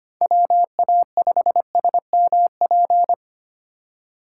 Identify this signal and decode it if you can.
Morse code, heard as a keyed tone, about 25 wpm, 700 Hz. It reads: WA5SMP
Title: USA Call Signs - 2x Character Spacing - 25wpm